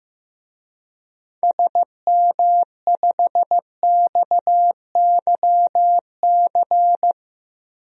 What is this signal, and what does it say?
Morse code, heard as a keyed tone, about 15 wpm, 700 Hz. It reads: SM5XYC
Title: DXCC-100 Call Signs - 6 Characters - 15wpm